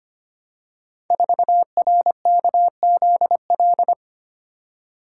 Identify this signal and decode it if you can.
Morse code, heard as a keyed tone, about 25 wpm, 700 Hz. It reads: 4RKZL
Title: Code Groups - 5 Letter:Numbers - 25wpm